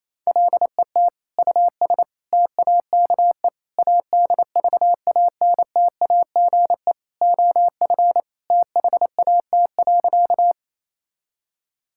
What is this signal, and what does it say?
Morse code, heard as a keyed tone, about 28 wpm, 700 Hz. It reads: LET US TAKE ADVANTAGE OF THAT.
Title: Sentences from Top 1000 Words - Review Part 6 - 28wpm